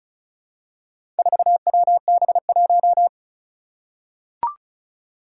Morse code, heard as a keyed tone, about 34 wpm, 700 Hz. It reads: 4WB1
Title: Four Challenging Letter-Number Combinations 35wpm